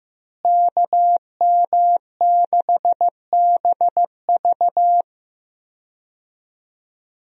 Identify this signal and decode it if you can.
Morse code, heard as a keyed tone, about 16 wpm, 700 Hz. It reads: KM6BV
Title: USA Call Signs - 5 Characters - Repeated 2x - 15wpm